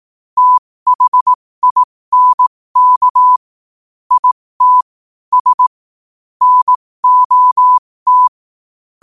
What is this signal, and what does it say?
Morse code, heard as a keyed tone, about 18 wpm, spaced as for 15 wpm, 1000 Hz. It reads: THINK IT S NOT
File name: qotdfiles / 15wpm